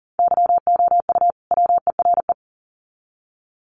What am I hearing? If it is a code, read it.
Morse code, heard as a keyed tone, about 40 wpm, 700 Hz. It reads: YOU WERE
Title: Sets of 2 Words 40wpm